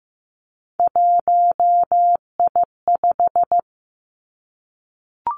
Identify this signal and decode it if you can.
Morse code, heard as a keyed tone, about 15 wpm, 700 Hz. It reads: 1I5
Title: Three Challenging Letter-Number Combinations 15wpm